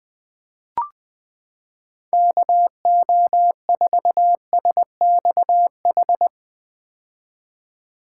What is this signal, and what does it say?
Morse code, heard as a keyed tone, about 21 wpm, 700 Hz. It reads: KO4SXH
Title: 2023 POTA Hunter Call Signs - 20wpm